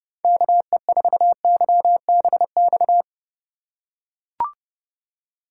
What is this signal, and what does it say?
Morse code, heard as a keyed tone, about 30 wpm, 700 Hz. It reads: KE4YBX
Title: USA Call Signs - 6 Characters - 2x Character Spacing - 30wpm